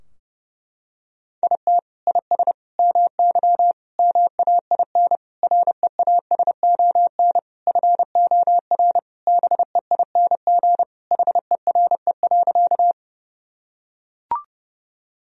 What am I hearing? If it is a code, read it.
Morse code, heard as a keyed tone, about 30 wpm, 700 Hz. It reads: IT IS MY MAIN REASON FOR BEING HERE.
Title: Sentences from Top 1000 Words - Review Part 2 - 30wpm